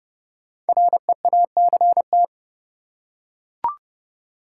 Morse code, heard as a keyed tone, about 31 wpm, 700 Hz. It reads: REACT
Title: Five Letter Words - 30wpm